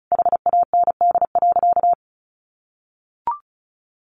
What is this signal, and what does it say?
Morse code, heard as a keyed tone, about 34 wpm, 700 Hz. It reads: HAND.
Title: Sentences from Top 1200 Words 35wpm